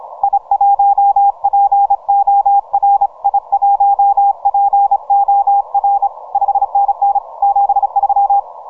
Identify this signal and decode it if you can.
Morse code, heard as a keyed tone, about 28 wpm, 800 Hz, with radio noise. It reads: I1POR I1POR 5NN 73
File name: VK0EK 12MCW